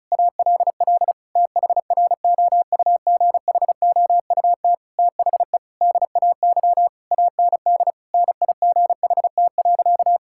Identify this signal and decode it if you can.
Morse code, heard as a keyed tone, about 36 wpm, 700 Hz. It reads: ALL THROUGHOUT THE DAY AND NIGHT.
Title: Sentences from Top 1000 Words - Review Part 6 - 35wpm